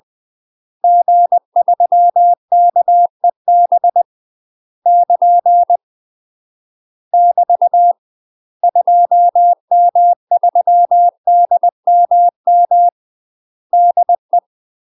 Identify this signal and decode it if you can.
Morse code, heard as a keyed tone, about 20 wpm, 700 Hz. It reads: G3KEB <KN> <BT> 2M3DMM DE
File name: UK MASTER QSOs TRAINING / 20WPM / UK MASTER QSOs 0053